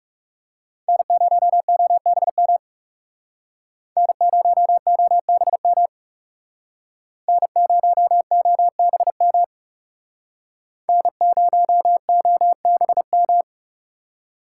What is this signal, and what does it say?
Morse code, heard as a keyed tone, about 37 wpm, 700 Hz. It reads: N0OBM N0OBM N0OBM N0OBM
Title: Speed Racing - USA 1x3 Call Signs 30wpm